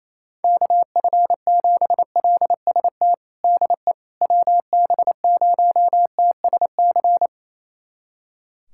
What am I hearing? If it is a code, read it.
Morse code, heard as a keyed tone, about 28 wpm, 700 Hz. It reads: KF7LST DE WB0TSC
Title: USA Callsign DE Callsign 28wpm